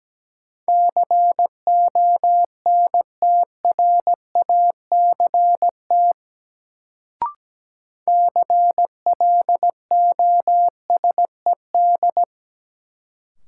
Morse code, heard as a keyed tone, about 17 wpm, 700 Hz. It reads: CONTRACT CLOSED
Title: Sentences from Top 1400 Words - New Word Drill - 17wpm